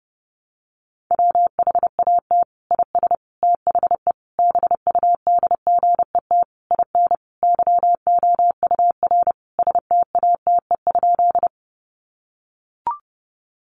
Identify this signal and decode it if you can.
Morse code, heard as a keyed tone, about 30 wpm, 700 Hz. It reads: WHAT IS THE BUDGET IN YOUR STATE?
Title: Sentences from Top 1200 Words 30wpm